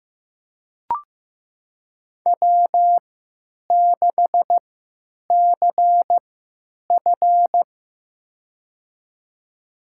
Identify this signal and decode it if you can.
Morse code, heard as a keyed tone, about 15 wpm, spaced as for 9 wpm, 700 Hz. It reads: W6CF
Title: USA Call Signs - 4 Characters - 3x Character Spacing - 15wpm